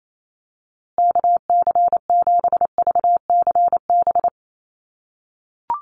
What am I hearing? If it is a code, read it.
Morse code, heard as a keyed tone, about 28 wpm, 700 Hz. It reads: KC7VCB
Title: USA Call Signs Repeated 2x 28wpm